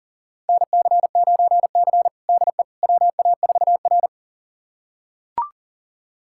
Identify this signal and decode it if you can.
Morse code, heard as a keyed tone, about 40 wpm, 700 Hz. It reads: NC9C DE WA4R
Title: USA Callsign DE Callsign - 4 Characters - 40wpm